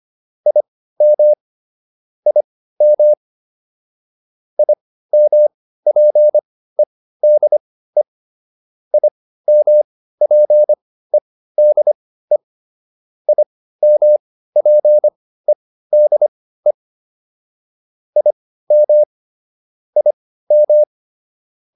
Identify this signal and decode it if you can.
Morse code, heard as a keyed tone, about 25 wpm, spaced as for 15 wpm, 600 Hz. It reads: IM IM IMPEDE IMPEDE IMPEDE IM IM
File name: IM6 15